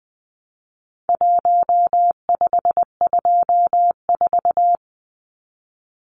Morse code, heard as a keyed tone, about 20 wpm, 700 Hz. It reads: 1524
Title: Numbers - 4-Digit 20wpm